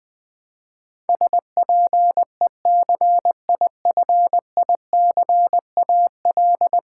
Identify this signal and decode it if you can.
Morse code, heard as a keyed tone, about 20 wpm, 700 Hz. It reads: SPECIFICAL
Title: Build A Word - 5 Letters At A Time - 12-Letter Words - 20wpm